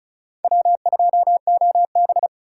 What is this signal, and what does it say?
Morse code, heard as a keyed tone, about 36 wpm, 690 Hz. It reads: W2OB
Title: USA Call Signs - 4 Characters - 3x Character Spacing - 35wpm